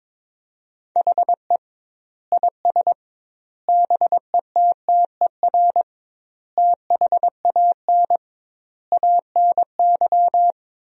Morse code, heard as a keyed tone, about 21 wpm, 700 Hz. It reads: HE IS BETTER THAN ANY
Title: 2x Word-Spacing - Sentences from Top 400 Words - 22wpm